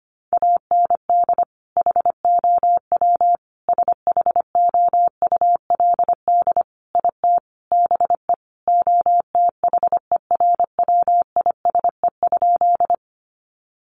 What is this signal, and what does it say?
Morse code, heard as a keyed tone, about 25 wpm, 700 Hz. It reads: AND HOW SHOULD IT BE OTHERWISE?
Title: Sentences from Top 1200 Words 25wpm